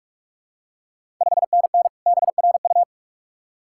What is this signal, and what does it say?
Morse code, heard as a keyed tone, about 44 wpm, 690 Hz. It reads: 5NN BRU
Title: Ontario QSO Party Exchange 45wpm